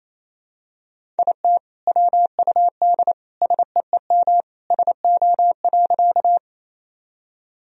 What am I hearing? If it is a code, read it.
Morse code, heard as a keyed tone, about 28 wpm, 700 Hz. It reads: IT WUD SEEM SO.
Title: Sentences from Top 300 Words and Abbreviations 28wpm